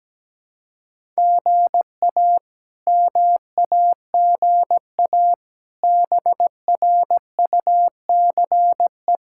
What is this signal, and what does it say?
Morse code, heard as a keyed tone, about 17 wpm, 700 Hz. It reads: GA MAGA BRUCE
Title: K1USN SST QSO Practice - 17wpm